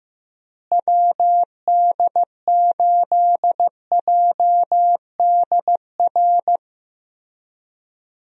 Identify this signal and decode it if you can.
Morse code, heard as a keyed tone, about 15 wpm, 700 Hz. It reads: WD8JDR
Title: USA Call Signs - 6 Characters - Repeated 3x - 15wpm